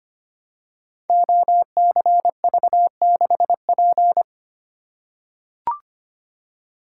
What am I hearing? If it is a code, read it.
Morse code, heard as a keyed tone, about 25 wpm, 700 Hz. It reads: OCV6P
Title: Code Groups - 5 Letter:Numbers - 25wpm